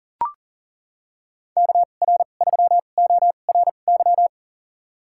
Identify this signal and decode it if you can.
Morse code, heard as a keyed tone, about 40 wpm, spaced as for 29 wpm, 700 Hz. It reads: KR3ORY
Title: USA Call Signs - 6 Characters - 2x Character Spacing - 40wpm